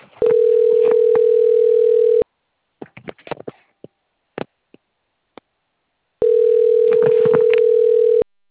Ringback tone (North America), 440 + 480 Hz.